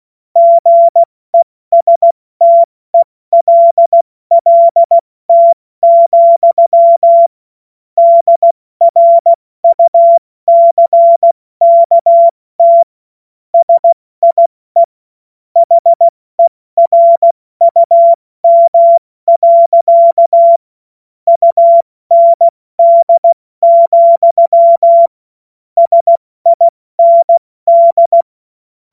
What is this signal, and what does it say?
Morse code, heard as a keyed tone, about 16 wpm, spaced as for 14 wpm, 680 Hz. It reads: GESTELLT, DRUCKT SIE HERUM. UND, SIND
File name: Morsekurs / Witze / Witz 12 März 2024 70 F80